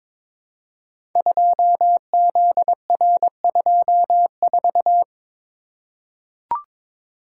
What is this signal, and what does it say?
Morse code, heard as a keyed tone, about 22 wpm, 700 Hz. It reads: 2ZR24
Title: Code Groups - 5 Letter:Numbers - 22wpm